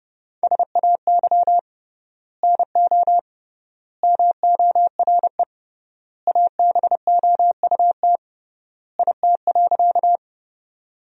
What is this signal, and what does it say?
Morse code, heard as a keyed tone, about 30 wpm, 700 Hz. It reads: SAY NO MORE ABOUT IT.
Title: 3x Word-Spacing - Sentences from Top 200 Words - 30wpm